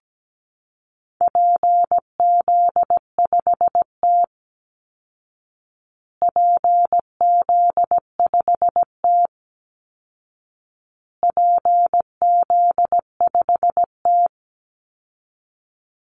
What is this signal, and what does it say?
Morse code, heard as a keyed tone, about 17 wpm, 700 Hz. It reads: PZ5T PZ5T PZ5T
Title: Call Signs Repeated 2x - 17wpm